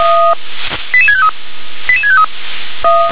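Five-tone selective call (ZVEI), heard as quick runs of five tones, tones of about 70 ms. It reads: 88653 88653
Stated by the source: ZVEI five-tone selective call received on SDR.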